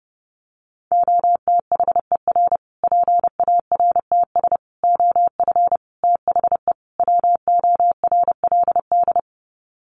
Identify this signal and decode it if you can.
Morse code, heard as a keyed tone, about 30 wpm, 700 Hz. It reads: OTHER PARTS OF THE WORLD
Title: Sets of 5 Words Encore 30wpm